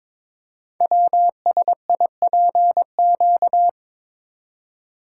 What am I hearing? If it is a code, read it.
Morse code, heard as a keyed tone, about 22 wpm, 700 Hz. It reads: WSIPQ
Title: Code Groups - 5 Letters - 22wpm